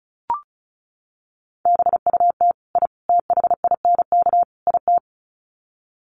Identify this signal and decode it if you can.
Morse code, heard as a keyed tone, about 34 wpm, 700 Hz. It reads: BUT I THINK IT
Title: Sets of 4 Words 35wpm